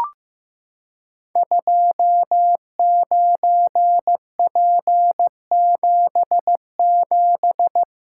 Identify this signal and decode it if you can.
Morse code, heard as a keyed tone, about 15 wpm, 700 Hz. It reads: 29P77
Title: Code Groups - 5 Letter:Numbers - 15wpm